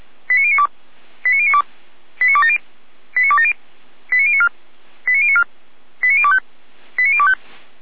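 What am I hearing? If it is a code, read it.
Five-tone selective call (ZVEI), heard as quick runs of five tones, tones of about 70 ms. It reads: 89092 89092 89270 89270 89094 89094 89025 89025
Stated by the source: ZVEI five-tone call 89020 received by SDR.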